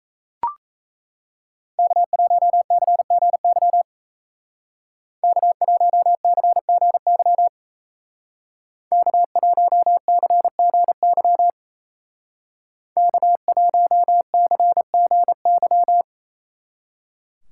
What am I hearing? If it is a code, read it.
Morse code, heard as a keyed tone, about 33 wpm, 700 Hz. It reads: K1CGY K1CGY K1CGY K1CGY
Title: Speed Racing - USA 1x3 Call Signs 28wpm